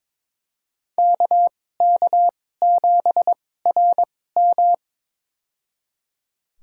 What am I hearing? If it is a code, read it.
Morse code, heard as a keyed tone, about 21 wpm, spaced as for 16 wpm, 700 Hz. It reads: KK7RM
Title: USA Call Signs - 5 Characters - 2x Character Spacing - 22wpm